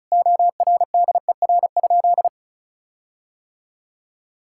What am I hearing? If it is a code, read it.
Morse code, heard as a keyed tone, about 36 wpm, 700 Hz. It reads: ORDER?
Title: Sentences from Top 1600 Words 35wpm